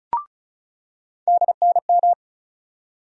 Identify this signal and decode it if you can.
Morse code, heard as a keyed tone, about 34 wpm, 700 Hz. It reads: DNM
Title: USA Call Sign Postfixes - 3 Letters 35wpm